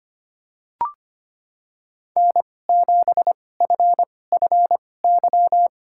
Morse code, heard as a keyed tone, about 25 wpm, spaced as for 18 wpm, 700 Hz. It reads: N7FFY
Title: USA Call Signs - 2x Character Spacing - 25wpm